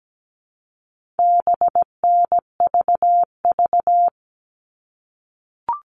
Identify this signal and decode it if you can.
Morse code, heard as a keyed tone, about 17 wpm, 700 Hz. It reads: BNVV
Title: All Cut Numbers - 4-Digit 17wpm